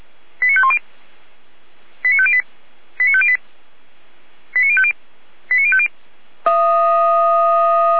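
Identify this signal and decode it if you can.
Five-tone selective call (ZVEI), heard as quick runs of five tones, tones of about 70 ms. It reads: 89510 89508 89508 89050 89050
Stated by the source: ZVEI five-tone selective call received on an SDR.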